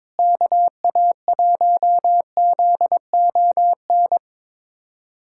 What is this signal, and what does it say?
Morse code, heard as a keyed tone, about 22 wpm, 700 Hz. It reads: KA1ZON
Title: USA Callsign DE Callsign - 6 Characters - 22wpm